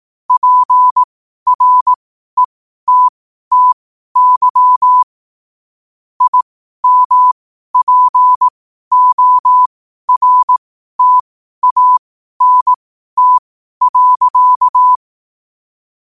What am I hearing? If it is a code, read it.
Morse code, heard as a keyed tone, about 18 wpm, spaced as for 12 wpm, 1000 Hz. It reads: PRETTY IMPORTANT.
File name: qotdfiles / 12wpm